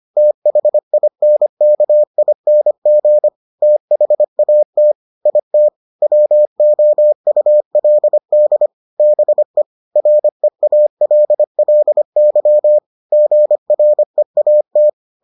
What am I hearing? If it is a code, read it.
Morse code, heard as a keyed tone, about 25 wpm, 600 Hz. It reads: THINKING THAT IT WOULD BE REALLY GREAT